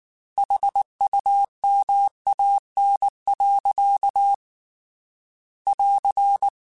Morse code, heard as a keyed tone, about 19 wpm, 800 Hz. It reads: HUMAN. <AR>